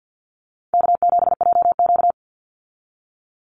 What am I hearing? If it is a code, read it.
Morse code, heard as a keyed tone, about 50 wpm, 700 Hz. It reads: K7JQ
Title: Easy Call Signs Repeated 2x - 50wpm